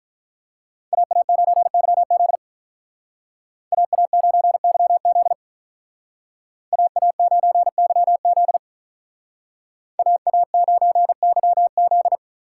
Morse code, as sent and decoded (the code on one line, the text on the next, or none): .- .- ----. -.-- --.. / .- .- ----. -.-- --.. / .- .- ----. -.-- --.. / .- .- ----. -.-- --..
AA9YZ AA9YZ AA9YZ AA9YZ